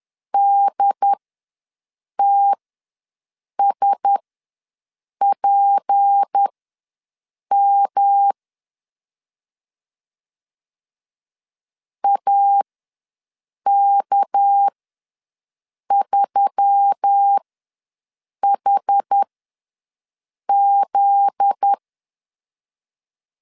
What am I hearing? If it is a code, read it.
Morse code, heard as a keyed tone, about 10 wpm, spaced as for 6 wpm, 800 Hz. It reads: DTSPM AK3HZ
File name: LEZIONI / 9 2 ws 10wpm 4130